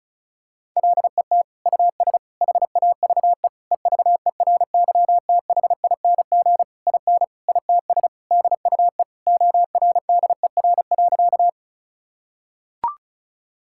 Morse code, heard as a keyed tone, about 36 wpm, 700 Hz. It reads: LET US HAVE EVERYTHING IN ITS DUE ORDER.
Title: Sentences from Top 1000 Words - Review Part 6 - 35wpm